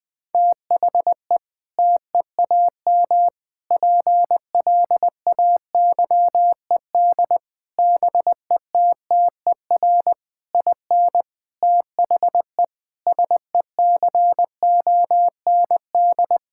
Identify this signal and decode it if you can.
Morse code, heard as a keyed tone, about 20 wpm, 700 Hz. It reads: THE TEAM PLAYED BETTER IN THE SECOND
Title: Sentences with 6-Word Phrases - 20wpm